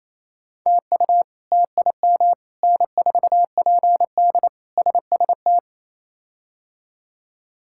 Morse code, heard as a keyed tone, about 27 wpm, 700 Hz. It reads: TU TIM N4PD SST
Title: K1USN SST QSO Practice - 28wpm